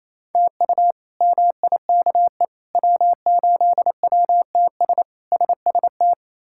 Morse code, heard as a keyed tone, about 27 wpm, 700 Hz. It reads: TU MIKE W8WTS SST
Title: K1USN SST QSO Practice - 28wpm